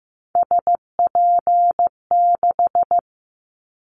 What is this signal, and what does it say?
Morse code, heard as a keyed tone, about 15 wpm, 700 Hz. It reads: SP6
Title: Callsign Prefixes for Top 25 DXCC - 15wpm